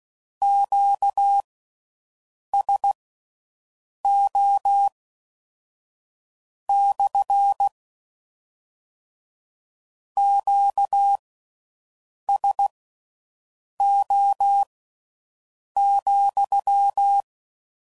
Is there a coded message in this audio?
Yes: Morse code from 0.4 to 17.2 s, heard as a keyed tone, about 16 wpm, spaced as for 6 wpm, 800 Hz. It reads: QSO / QSO,